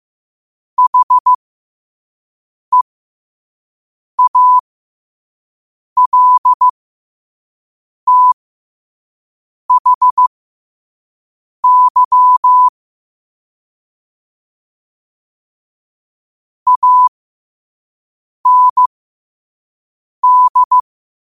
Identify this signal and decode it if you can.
Morse code, heard as a keyed tone, about 15 wpm, spaced as for 5 wpm, 1000 Hz. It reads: HEALTHY AND
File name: qotdfiles / 5wpm